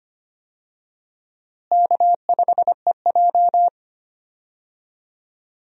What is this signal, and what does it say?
Morse code, heard as a keyed tone, about 25 wpm, 700 Hz. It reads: K5EJ
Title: USA Call Signs - 4 Characters - Repeated 3x - 25wpm